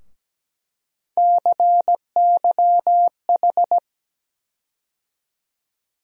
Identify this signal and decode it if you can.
Morse code, heard as a keyed tone, about 17 wpm, 700 Hz. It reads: CYH
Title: USA Call Sign Postfixes - 3 Letters 17wpm